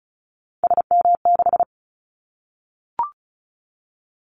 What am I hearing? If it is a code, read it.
Morse code, heard as a keyed tone, about 34 wpm, 700 Hz. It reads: SM6
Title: Callsign Prefixes for Top 25 DXCC - Repeated 3x - 35wpm